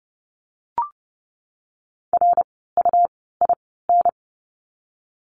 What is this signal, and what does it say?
Morse code, heard as a keyed tone, about 30 wpm, spaced as for 17 wpm, 700 Hz. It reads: RUIN
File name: Lesson 17f - Letter u - Words 30wpm